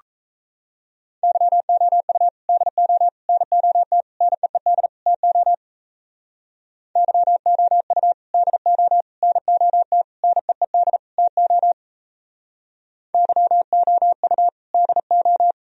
Morse code, heard as a keyed tone, about 36 wpm, 700 Hz. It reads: YOU DO NOT NEED TO YOU DO NOT NEED TO YOU DO